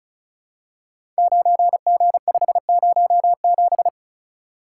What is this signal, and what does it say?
Morse code, heard as a keyed tone, about 36 wpm, 700 Hz. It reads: 9G507
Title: Code Groups - 5 Letter:Numbers - 35wpm